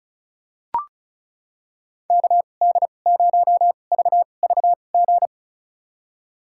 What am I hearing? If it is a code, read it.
Morse code, heard as a keyed tone, about 36 wpm, spaced as for 25 wpm, 700 Hz. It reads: KD0VVG
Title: USA Call Signs - 2x Character Spacing - 35wpm